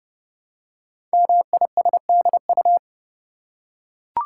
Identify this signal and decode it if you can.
Morse code, heard as a keyed tone, about 30 wpm, 690 Hz. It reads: MISDU
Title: Code Groups - 5 Letters - 30wpm